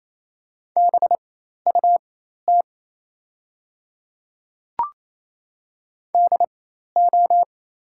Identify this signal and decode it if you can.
Morse code, heard as a keyed tone, about 27 wpm, spaced as for 13 wpm, 690 Hz. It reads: BUT DO
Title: 4x Character Spacing - Top 200 Words - 28wpm